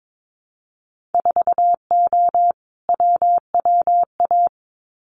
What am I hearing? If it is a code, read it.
Morse code, heard as a keyed tone, about 22 wpm, 700 Hz. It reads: 4O WWA
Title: Winter Field Day Exchanges 22wpm